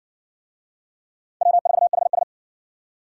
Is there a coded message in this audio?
Yes: Morse code from 1.4 to 2.2 s, heard as a keyed tone, about 58 wpm, 700 Hz. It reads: W4HS